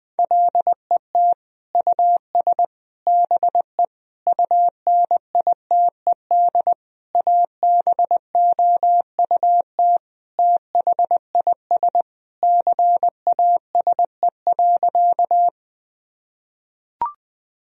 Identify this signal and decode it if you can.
Morse code, heard as a keyed tone, about 21 wpm, 700 Hz. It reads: LET US BE UNITED ABOUT THIS CASE.
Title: Sentences from Top 1000 Words - Review Part 4 - 20wpm